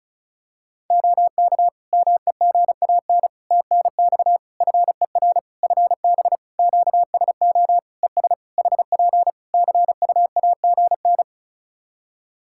Morse code, heard as a keyed tone, about 34 wpm, 700 Hz. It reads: OK MEGAN TNX FER FB QSO ES HP CUAGN
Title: Practice QSO - Exchange 4 - Ending - 35wpm